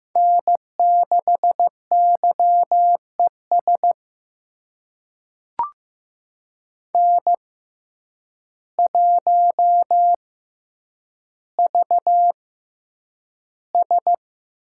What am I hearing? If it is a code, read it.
Morse code, heard as a keyed tone, about 15 wpm, spaced as for 8 wpm, 700 Hz. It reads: N6YES N1VS
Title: USA Call Signs - 5 Characters - 6x Character Spacing - 15wpm